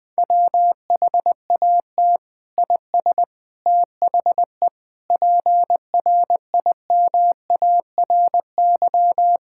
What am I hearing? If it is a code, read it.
Morse code, heard as a keyed tone, about 20 wpm, 700 Hz. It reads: WHAT IS THE PRIMARY
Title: Sentences from Top 1000 Words - Review Part 5 - 20wpm